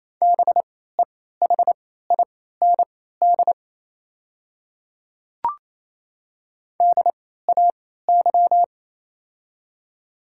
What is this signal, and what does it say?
Morse code, heard as a keyed tone, about 28 wpm, spaced as for 16 wpm, 700 Hz. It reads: BEHIND DAY
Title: 3x Character Spacing - Top 500 Words - 28wpm